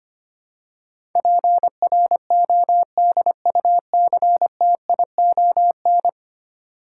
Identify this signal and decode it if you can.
Morse code, heard as a keyed tone, about 25 wpm, 700 Hz. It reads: PRODUCTION
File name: 10 Letter Words - First 3 Letters Given - 25wpm